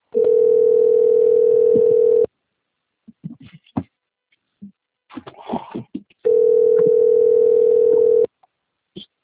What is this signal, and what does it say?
Ringback tone (North America), 440 + 480 Hz.